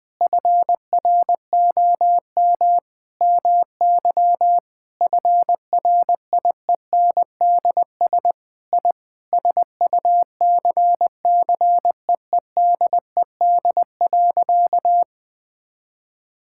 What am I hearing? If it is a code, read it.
Morse code, heard as a keyed tone, about 20 wpm, 700 Hz. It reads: FROM MY FRIENDS I SUCCEEDED.
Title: Sentences with 7-Word Phrases - 20wpm